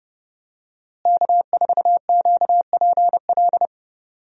Morse code, heard as a keyed tone, about 30 wpm, 700 Hz. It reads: K4QPL
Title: USA Call Signs - 5 Characters - Repeated 2x - 30wpm